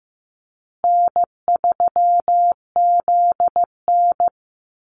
Morse code, heard as a keyed tone, about 15 wpm, 700 Hz. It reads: N3ZN
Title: Call Signs Repeated 2x - 15wpm